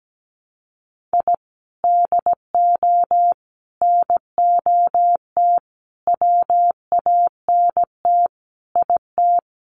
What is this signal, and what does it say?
Morse code, heard as a keyed tone, about 17 wpm, 700 Hz. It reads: I DO NOT WANT IT
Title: Sets of 5 Words Encore2 17wpm